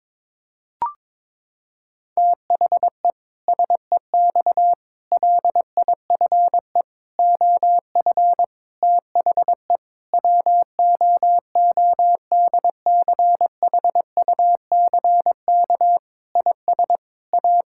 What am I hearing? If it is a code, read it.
Morse code, heard as a keyed tone, about 22 wpm, 700 Hz. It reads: THE SEX LIFE OF THE WOODCHUCK IS A
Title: Pangram Challenge v2 - 22wpm